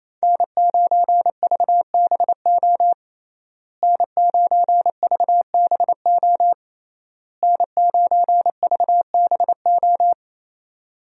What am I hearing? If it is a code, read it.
Morse code, heard as a keyed tone, about 28 wpm, 700 Hz. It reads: N9VBO N9VBO N9VBO
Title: USA Call Signs - 5 Characters - Repeated 3x - 28wpm